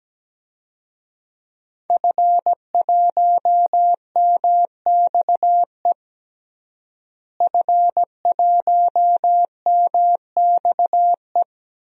Morse code, heard as a keyed tone, about 17 wpm, 700 Hz. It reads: F1MXE F1MXE
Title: DXCC-100 Call Signs - 5 Characters - Repeated 3x - 17wpm